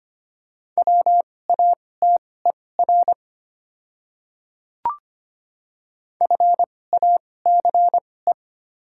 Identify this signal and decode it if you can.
Morse code, heard as a keyed tone, about 25 wpm, spaced as for 18 wpm, 700 Hz. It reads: WATER FACE
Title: 2x Character Spacing - Top 500 Words - 25wpm